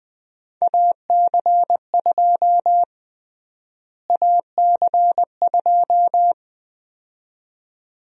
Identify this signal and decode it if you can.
Morse code, heard as a keyed tone, about 20 wpm, 700 Hz. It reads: AC2 AC2
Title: USA Call Sign Prefixes - Repeated 2x - 20wpm